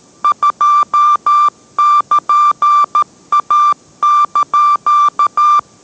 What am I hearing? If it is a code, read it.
Morse code, heard as a keyed tone, about 13 wpm, 1180 Hz. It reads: 2<KN>A)